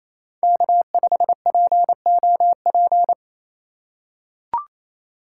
Morse code, heard as a keyed tone, about 27 wpm, 700 Hz. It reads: K5POP
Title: USA Call Signs - 6x Character Spacing - 28wpm